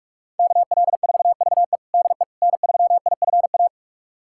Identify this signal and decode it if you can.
Morse code, heard as a keyed tone, about 46 wpm, 700 Hz. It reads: KL4VE DE N3IFA
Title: USA Callsign DE Callsign - 5 Characters - 45wpm